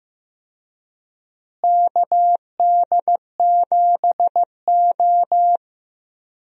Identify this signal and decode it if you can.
Morse code, heard as a keyed tone, about 16 wpm, 700 Hz. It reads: KD7O
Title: USA Call Signs - 4 Characters - Repeated 3x - 15wpm